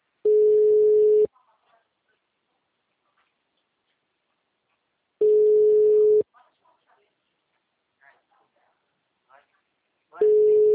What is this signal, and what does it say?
Ringback tone (Europe, 425 Hz), with speech.
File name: out- -2041